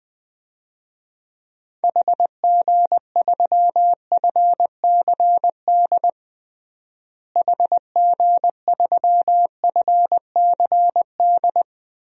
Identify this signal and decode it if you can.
Morse code, heard as a keyed tone, about 21 wpm, 700 Hz. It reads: HG3FCD HG3FCD
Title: DXCC-100 Call Signs - 6 Characters - Repeated 2x - 20wpm